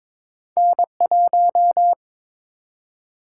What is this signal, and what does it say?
Morse code, heard as a keyed tone, about 22 wpm, 700 Hz. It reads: N1
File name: Two Letter-Number Combinations - 22wpm